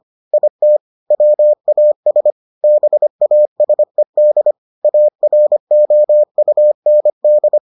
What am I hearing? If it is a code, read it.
Morse code, heard as a keyed tone, about 25 wpm, 600 Hz. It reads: IT WAS BASED AROUND